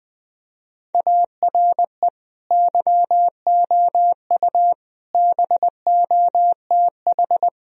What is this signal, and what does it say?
Morse code, heard as a keyed tone, about 20 wpm, 700 Hz. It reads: ARE YOU BOTH
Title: Sentences from Top 1000 Words - Review Part 1 - 20wpm